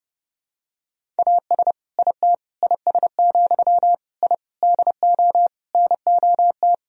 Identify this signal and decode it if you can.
Morse code, heard as a keyed tone, about 30 wpm, 700 Hz. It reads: AS IT IS, I DO NOT
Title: Sentences from Top 1000 Words - Review Part 2 - 30wpm